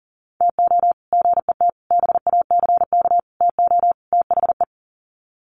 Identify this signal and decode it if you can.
Morse code, heard as a keyed tone, about 40 wpm, 700 Hz. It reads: TO GET BACK TO THE
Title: Sets of 5 Words Encore 40wpm